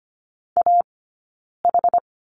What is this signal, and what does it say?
Morse code, heard as a keyed tone, about 25 wpm, 700 Hz. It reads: A H
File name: course / 25:10wpm / 092 Lesson 16d - Letter c - Two Characters 25wpm